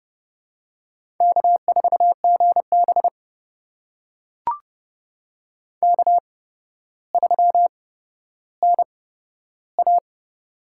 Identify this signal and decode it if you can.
Morse code, heard as a keyed tone, about 30 wpm, spaced as for 13 wpm, 700 Hz. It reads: K4GB K3NA
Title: USA Call Signs - 8x Character Spacing - 30wpm